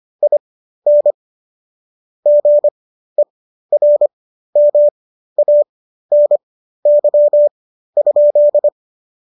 Morse code, heard as a keyed tone, about 25 wpm, spaced as for 13 wpm, 570 Hz. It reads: IN GERMANY?